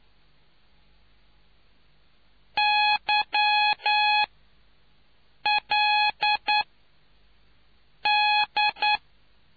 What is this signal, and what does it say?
Morse code, heard as a keyed tone, about 9 wpm, spaced as for 5 wpm, 840 Hz. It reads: YLD